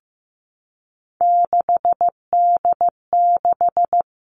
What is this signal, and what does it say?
Morse code, heard as a keyed tone, about 15 wpm, 700 Hz. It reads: 6D6
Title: All Cut Numbers - 3-Digit 15wpm